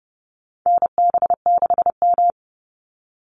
Morse code, heard as a keyed tone, about 30 wpm, 700 Hz. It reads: NB6M
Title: Call Signs Repeated 2x - 30wpm